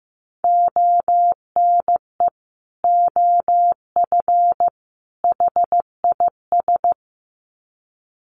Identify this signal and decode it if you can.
Morse code, heard as a keyed tone, about 15 wpm, 700 Hz. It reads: ONE OF HIS
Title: Sets of 3 Words 15wpm